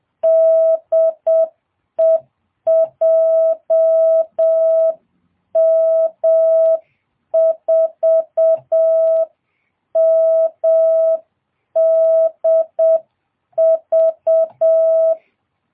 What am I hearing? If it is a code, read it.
Morse code, heard as a keyed tone, about 7 wpm, 660 Hz. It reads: DE JM4MDV